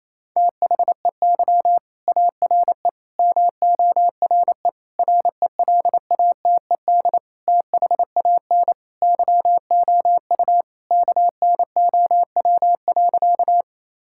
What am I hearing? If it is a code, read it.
Morse code, heard as a keyed tone, about 28 wpm, 700 Hz. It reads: THEY ARE MORE RELATED THAN YOU KNOW.
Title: Sentences from Top 1000 Words - Review Part 5 - 28wpm